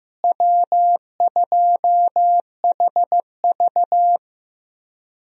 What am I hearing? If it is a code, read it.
Morse code, heard as a keyed tone, about 15 wpm, 700 Hz. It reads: W2HV
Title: USA Call Signs - 4 Characters - Repeated 3x - 15wpm